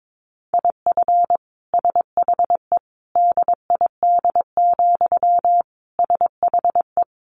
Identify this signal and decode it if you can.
Morse code, heard as a keyed tone, about 22 wpm, 700 Hz. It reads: IF SHE DID, SHE
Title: Sentences from Top 1400 Words 22wpm